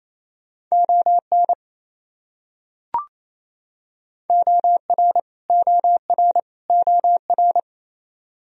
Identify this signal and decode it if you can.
Morse code, heard as a keyed tone, about 28 wpm, 700 Hz. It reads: ON OR OR OR
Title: Top 100 Words Repeated 28wpm